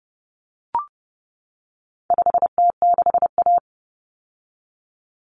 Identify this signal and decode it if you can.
Morse code, heard as a keyed tone, about 30 wpm, 700 Hz. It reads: 5T6A
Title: Common Cut Numbers - 4-Digit 30wpm